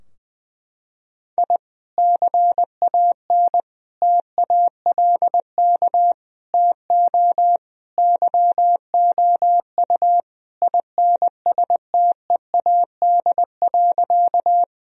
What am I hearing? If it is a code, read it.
Morse code, heard as a keyed tone, about 21 wpm, 700 Hz. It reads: I CAN TALK TO YOU INSTEAD.
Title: Sentences from Top 1000 Words - Review Part 4 - 20wpm